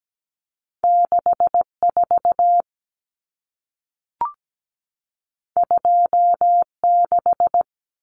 Morse code, heard as a keyed tone, about 17 wpm, 700 Hz. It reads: 64 26
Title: Common Cut Numbers - 2-Digit 17wpm